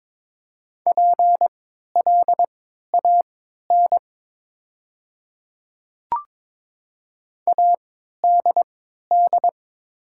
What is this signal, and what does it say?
Morse code, heard as a keyed tone, about 22 wpm, spaced as for 13 wpm, 700 Hz. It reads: PLAN ADD
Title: 3x Character Spacing - Top 500 Words - 22wpm